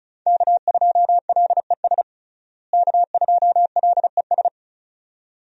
Morse code, heard as a keyed tone, about 36 wpm, 700 Hz. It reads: K2LES K2LES
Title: USA Call Signs - 5 Characters - Repeated 2x - 35wpm